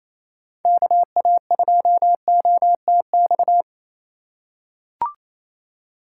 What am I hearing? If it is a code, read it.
Morse code, heard as a keyed tone, about 28 wpm, 700 Hz. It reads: KA2OTX
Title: USA Call Signs - 2x Character Spacing - 28wpm